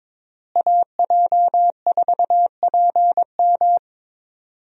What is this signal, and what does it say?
Morse code, heard as a keyed tone, about 22 wpm, 700 Hz. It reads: AJ4PM
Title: USA Call Signs - 5 Characters - Repeated 3x - 22wpm